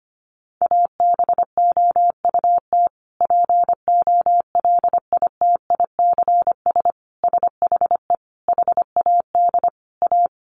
Morse code, heard as a keyed tone, about 25 wpm, 700 Hz. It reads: ABOUT POLITICS SHE HAD A
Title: Sentences from Top 1000 Words 25wpm